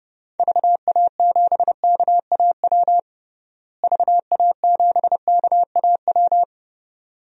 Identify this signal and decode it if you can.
Morse code, heard as a keyed tone, about 30 wpm, 700 Hz. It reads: VA7KAW VA7KAW
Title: DXCC-100 Call Signs - 6 Characters - Repeated 2x - 30wpm